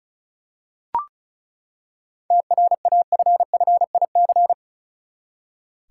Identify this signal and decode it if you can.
Morse code, heard as a keyed tone, about 34 wpm, 700 Hz. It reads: TRAFFIC
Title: Seven Letter Words - 35wpm